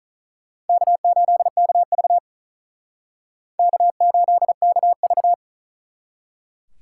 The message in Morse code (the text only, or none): K8KV K8KV